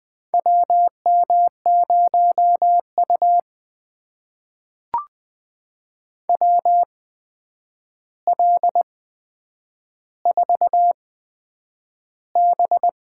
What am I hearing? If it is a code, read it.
Morse code, heard as a keyed tone, about 20 wpm, spaced as for 9 wpm, 700 Hz. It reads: WM0U WL4B
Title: USA Call Signs - 4 Characters - 8x Character Spacing - 20wpm